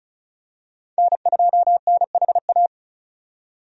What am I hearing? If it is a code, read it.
Morse code, heard as a keyed tone, about 36 wpm, 700 Hz. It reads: N2NHA
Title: USA Call Signs - 5 Characters - 35wpm